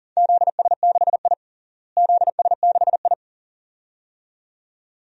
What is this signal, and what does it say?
Morse code, heard as a keyed tone, about 40 wpm, 700 Hz. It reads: ZS6I ZS6I
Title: DXCC-100 Call Signs - 4 Characters - Repeated 2x - 40wpm